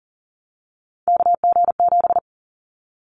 Morse code, heard as a keyed tone, about 40 wpm, 700 Hz. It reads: KG7
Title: Callsign Prefixes for Top 25 DXCC - Repeated 3x - 40wpm